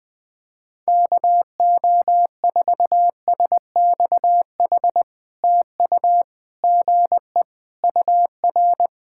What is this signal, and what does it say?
Morse code, heard as a keyed tone, about 20 wpm, 700 Hz. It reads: KO4SXH TU GE UR
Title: POTA QSO Practice - 20wpm